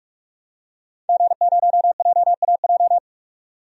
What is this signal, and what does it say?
Morse code, heard as a keyed tone, about 44 wpm, 700 Hz. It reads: G0JAJ
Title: DXCC-100 Call Signs - 5 Characters - 45wpm